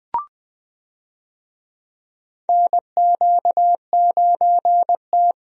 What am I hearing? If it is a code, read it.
Morse code, heard as a keyed tone, about 20 wpm, 700 Hz. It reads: NQ9T
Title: USA Call Signs - 4 Characters - Repeated 2x - 20wpm